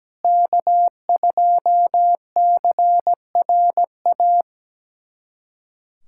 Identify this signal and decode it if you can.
Morse code, heard as a keyed tone, about 17 wpm, 700 Hz. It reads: K2CRA
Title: USA Callsign DE Callsign - 5 Characters - 17wpm